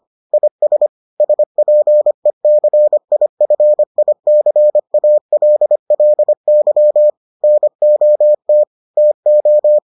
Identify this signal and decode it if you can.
Morse code, heard as a keyed tone, about 25 wpm, 600 Hz. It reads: IS SPECIFICALLY NOT TO